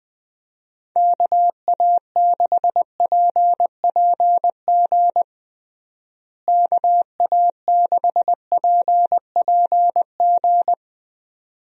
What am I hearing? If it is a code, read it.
Morse code, heard as a keyed tone, about 20 wpm, 700 Hz. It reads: KA6PPG KA6PPG
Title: USA Call Signs - 6 Characters - Repeated 2x - 20wpm